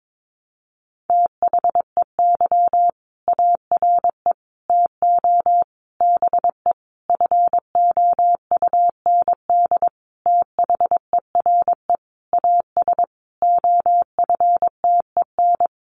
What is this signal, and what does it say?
Morse code, heard as a keyed tone, about 22 wpm, 700 Hz. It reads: THEY ARE TO BE FOUND THERE AS OFTEN
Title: Sentences from Top 1400 Words 22wpm